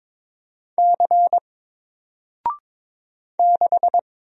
Morse code, heard as a keyed tone, about 22 wpm, 700 Hz. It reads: C 6
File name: Single-Letter-Number-22wpm